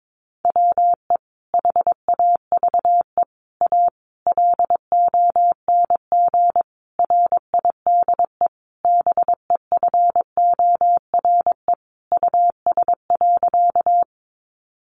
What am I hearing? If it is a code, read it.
Morse code, heard as a keyed tone, about 22 wpm, 700 Hz. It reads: WE HAVE A LONG RIDE BEFORE US.
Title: Sentences from Top 1200 Words 22wpm